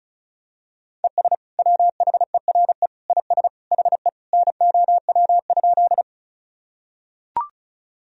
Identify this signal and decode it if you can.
Morse code, heard as a keyed tone, about 36 wpm, 700 Hz. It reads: ES WHERE IS HE NOW?
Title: Sentences from Top 200 Words and Abbreviations 35wpm